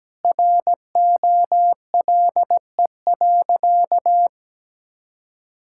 Morse code, heard as a keyed tone, about 17 wpm, 700 Hz. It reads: ROLE.
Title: Sentences from Top 1000 Words - Review Part 6 - 17wpm